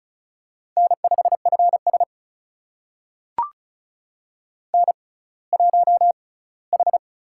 Morse code, heard as a keyed tone, about 36 wpm, spaced as for 20 wpm, 700 Hz. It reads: N5FS N1H
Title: USA Call Signs - 4 Characters - 6x Character Spacing - 35wpm